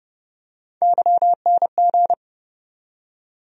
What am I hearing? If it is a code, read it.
Morse code, heard as a keyed tone, about 30 wpm, 700 Hz. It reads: YNG
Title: USA Call Sign Postfixes - 3 Letters 30wpm